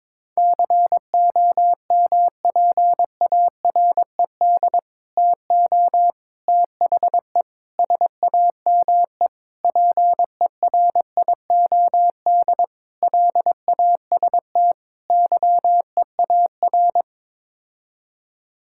Morse code, heard as a keyed tone, about 22 wpm, 700 Hz. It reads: COMPARED TO THE SAME PERIOD LAST YEAR
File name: practice / 22wpm / Sets of 7 Words - 22wpm